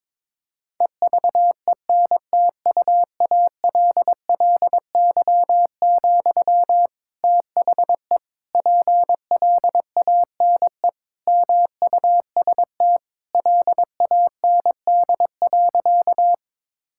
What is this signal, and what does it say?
Morse code, heard as a keyed tone, about 21 wpm, 700 Hz. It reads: EVENTUALLY, THE PLANE MUST LAND.
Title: Sentences from Top 1000 Words - Review Part 6 - 22wpm